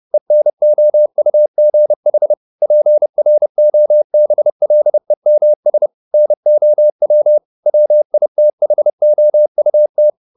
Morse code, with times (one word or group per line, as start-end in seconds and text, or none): 0.1-2.3 ENOUGH
2.6-5.9 PROBLEMS
6.1-7.4 NOW
7.7-10.1 WITHOUT